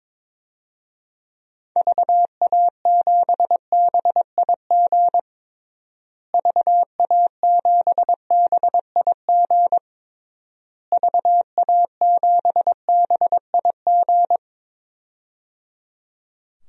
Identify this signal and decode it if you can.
Morse code, heard as a keyed tone, about 22 wpm, 700 Hz. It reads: VA7BIG VA7BIG VA7BIG
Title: DXCC-100 Call Signs - 6 Characters - Repeated 3x - 22wpm